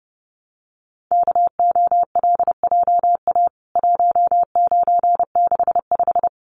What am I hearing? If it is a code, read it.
Morse code, heard as a keyed tone, about 29 wpm, 700 Hz. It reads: KOLJA 1965
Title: CWT Exchange 30wpm